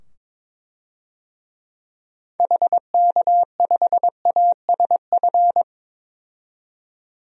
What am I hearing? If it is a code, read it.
Morse code, heard as a keyed tone, about 22 wpm, 700 Hz. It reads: HK5ASF
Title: DXCC-100 Call Signs - 6 Characters - 22wpm